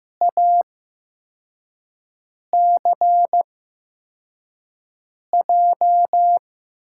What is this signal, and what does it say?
Morse code, heard as a keyed tone, about 15 wpm, spaced as for 4 wpm, 700 Hz. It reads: ACJ
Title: USA Call Signs - 6 Characters - 8x Character Spacing - 15wpm